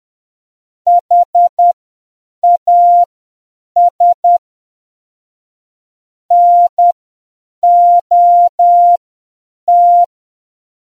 Morse code, heard as a keyed tone, about 10 wpm, spaced as for 7 wpm, 700 Hz. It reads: HAS NOT